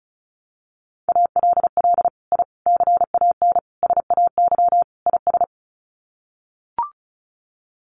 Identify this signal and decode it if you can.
Morse code, heard as a keyed tone, about 34 wpm, 700 Hz. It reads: ALL I CAN SAY IS
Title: Sets of 5 Words Encore2 35wpm